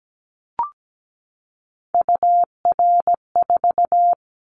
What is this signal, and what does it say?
Morse code, heard as a keyed tone, about 17 wpm, 700 Hz. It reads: UR4
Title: Callsign Prefixes for Top 25 DXCC - 17wpm